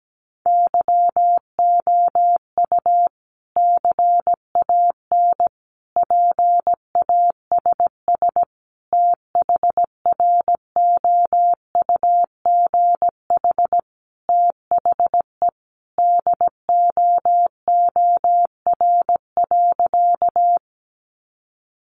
Morse code, heard as a keyed tone, about 17 wpm, 700 Hz. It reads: YOU CAN PASS THROUGH THE DOOR.
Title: Sentences from Top 400 Words 17wpm